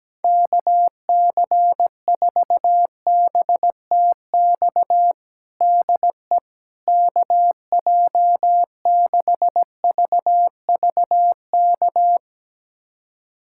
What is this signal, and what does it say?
Morse code, heard as a keyed tone, about 17 wpm, 700 Hz. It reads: KC4BTX DE KJ6VVK
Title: USA Callsign DE Callsign - 6 Characters - 17wpm